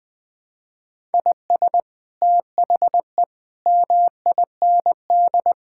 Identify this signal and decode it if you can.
Morse code, heard as a keyed tone, about 20 wpm, 700 Hz. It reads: IS THE MIND
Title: Sentences from Top 1000 Words - Review Part 6 - 20wpm